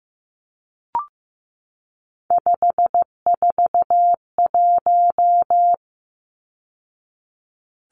Morse code, heard as a keyed tone, about 15 wpm, 700 Hz. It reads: 541
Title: Numbers - 3-Digit 15wpm